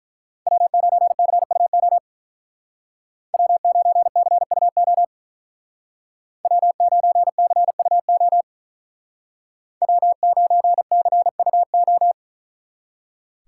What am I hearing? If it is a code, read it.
Morse code, heard as a keyed tone, about 44 wpm, 700 Hz. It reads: W9CUO W9CUO W9CUO W9CUO